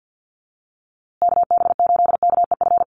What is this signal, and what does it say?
Morse code, heard as a keyed tone, about 48 wpm, 700 Hz. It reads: KB8KEF
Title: USA Call Signs Repeated 3x 50wpm